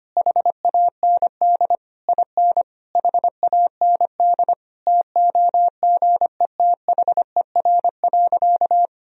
Morse code, heard as a keyed tone, about 25 wpm, 700 Hz. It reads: HAND IN HAND TOGETHER.
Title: Sentences from Top 1000 Words - Review Part 3 - 25wpm